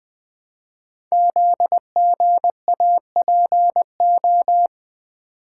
Morse code, heard as a keyed tone, about 20 wpm, 700 Hz. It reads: ZGAPO
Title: Code Groups - 5 Letters - 20wpm